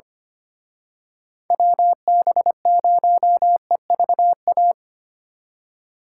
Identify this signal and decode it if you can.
Morse code, heard as a keyed tone, about 29 wpm, 700 Hz. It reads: WB0EVA
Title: Speed Racing - USA 2x3 Call Signs 25wpm